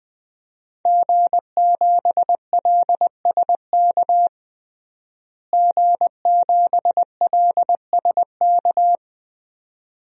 Morse code, heard as a keyed tone, about 21 wpm, 700 Hz. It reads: G7LSK G7LSK
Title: DXCC-100 Call Signs - 5 Characters - Repeated 2x - 20wpm